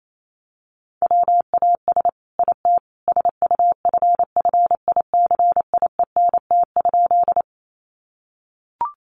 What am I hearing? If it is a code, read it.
Morse code, heard as a keyed tone, about 28 wpm, 700 Hz. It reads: WAS IT SUFFICIENT?
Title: Sentences from Top 1400 Words 28wpm